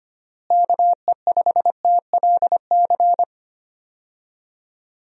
Morse code, heard as a keyed tone, about 25 wpm, 700 Hz. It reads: KE5TLC
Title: USA Call Signs - 6 Characters - Repeated 3x - 25wpm